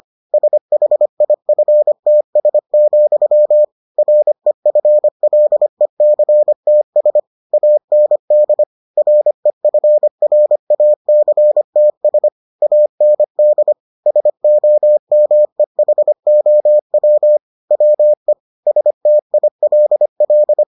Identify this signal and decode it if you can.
Morse code, heard as a keyed tone, about 25 wpm, 600 Hz. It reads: SHIFTS, REFLECTS AND REFRACTS AND SOMEHOW WE STILL